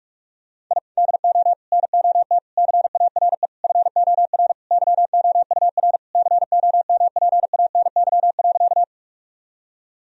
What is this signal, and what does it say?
Morse code, heard as a keyed tone, about 45 wpm, 700 Hz. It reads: I DO NOT CARE FOR YOUR COMPANY.
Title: Sentences from Top 1000 Words - Review Part 3 - 45wpm